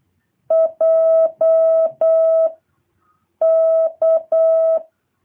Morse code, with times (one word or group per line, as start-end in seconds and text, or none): 0.5-2.5 J
3.4-4.8 K